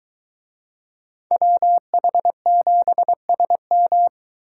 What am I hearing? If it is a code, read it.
Morse code, heard as a keyed tone, about 22 wpm, 700 Hz. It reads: WH7SM